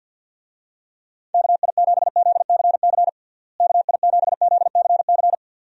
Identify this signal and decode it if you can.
Morse code, heard as a keyed tone, about 53 wpm, 700 Hz. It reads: KI7ZCC KI7ZCC
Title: USA Call Signs - 6 Characters - Repeated 2x - 50wpm